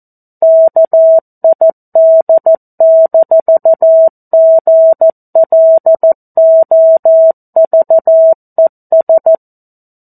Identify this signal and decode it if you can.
Morse code, heard as a keyed tone, about 14 wpm, 650 Hz. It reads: KID-GLOVES
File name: 02-Chapter-02 14 WPM